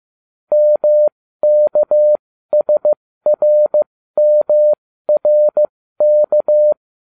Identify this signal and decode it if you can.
Morse code, heard as a keyed tone, about 15 wpm, spaced as for 13 wpm, 600 Hz. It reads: MKSRMRK